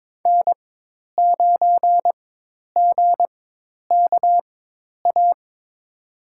Morse code, heard as a keyed tone, about 22 wpm, spaced as for 10 wpm, 700 Hz. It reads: N9GKA
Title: USA Call Signs - 4x Character Spacing - 22wpm